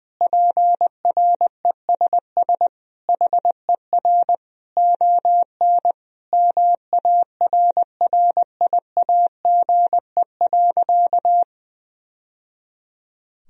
Morse code, heard as a keyed tone, about 20 wpm, 700 Hz. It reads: PRESS HER ON MARRIAGE.
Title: Sentences from Top 1000 Words - Review Part 1 - 20wpm